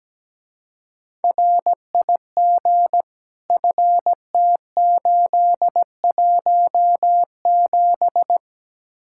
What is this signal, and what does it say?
Morse code, heard as a keyed tone, about 17 wpm, 700 Hz. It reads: RIG FT817
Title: Speed-Racing Common QSO Phrases - 17wpm